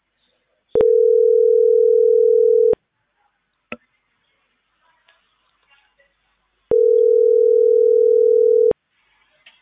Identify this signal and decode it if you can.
Ringback tone (North America), 440 + 480 Hz.